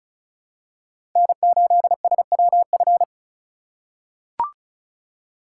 Morse code, heard as a keyed tone, about 36 wpm, 700 Hz. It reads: N8SWF
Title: USA Call Signs - 5 Characters - 4x Character Spacing - 35wpm